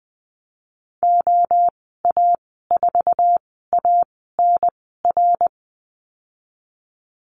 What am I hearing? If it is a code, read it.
Morse code, heard as a keyed tone, about 19 wpm, spaced as for 14 wpm, 700 Hz. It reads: OA4ANR
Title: 088 Lesson 15g - Number 5 - Call Signs 20wpm